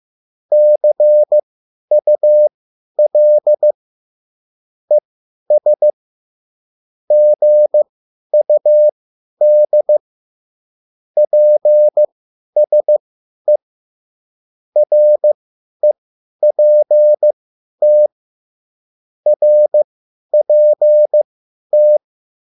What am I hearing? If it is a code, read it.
Morse code, heard as a keyed tone, about 15 wpm, spaced as for 10 wpm, 600 Hz. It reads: CUL ES GUD PSE REPT RPT